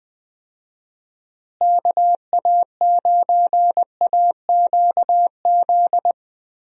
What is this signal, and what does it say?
Morse code, heard as a keyed tone, about 20 wpm, 700 Hz. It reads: KA9AQZ
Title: USA Call Signs - 6 Characters - Repeated 3x - 20wpm